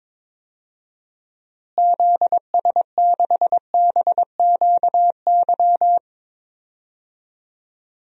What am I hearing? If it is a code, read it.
Morse code, heard as a keyed tone, about 22 wpm, 700 Hz. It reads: ZS6BQY
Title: DXCC-100 Call Signs - 6 Characters - Repeated 3x - 22wpm